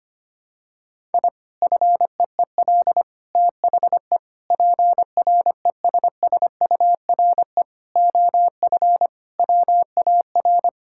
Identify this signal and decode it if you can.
Morse code, heard as a keyed tone, about 25 wpm, 700 Hz. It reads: I FEEL THE PRESSURE OF WAR
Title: Sentences from Top 1000 Words - Review Part 6 - 25wpm